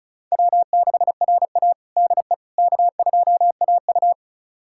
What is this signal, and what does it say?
Morse code, heard as a keyed tone, about 36 wpm, 700 Hz. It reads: W6RA DE K2AU
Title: USA Callsign DE Callsign - 4 Characters - 35wpm